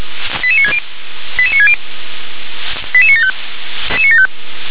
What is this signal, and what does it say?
Five-tone selective call (ZVEI), heard as quick runs of five tones, tones of about 70 ms. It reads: 88960 88960 88975
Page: SDR/ZVEI